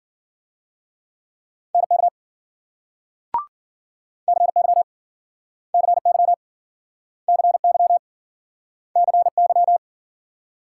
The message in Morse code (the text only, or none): NK CY CY CY CY